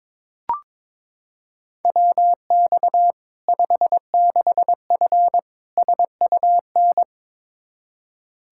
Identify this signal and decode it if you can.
Morse code, heard as a keyed tone, about 22 wpm, 700 Hz. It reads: WX 56F SUN
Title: Practice QSO - Exchange 2 - Info - 22wpm